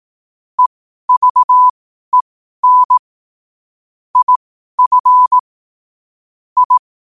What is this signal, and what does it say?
Morse code, heard as a keyed tone, about 18 wpm, spaced as for 12 wpm, 1000 Hz. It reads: EVEN IF I
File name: qotdfiles / 12wpm